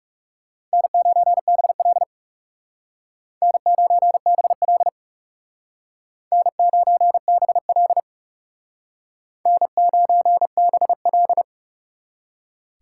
Morse code, heard as a keyed tone, about 36 wpm, 700 Hz. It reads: N9BL N9BL N9BL N9BL
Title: Speed Racing - USA 1x2 Call Signs 30wpm